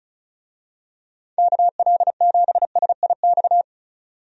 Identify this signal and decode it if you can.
Morse code, heard as a keyed tone, about 36 wpm, 690 Hz. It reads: KL7SIX
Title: DXCC-100 Call Signs - 6 Characters - 35wpm